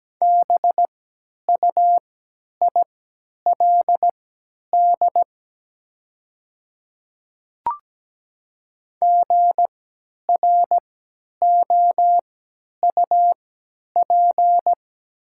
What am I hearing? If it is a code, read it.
Morse code, heard as a keyed tone, about 17 wpm, spaced as for 10 wpm, 700 Hz. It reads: BUILD GROUP
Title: 3x Character Spacing - Top 500 Words - 17wpm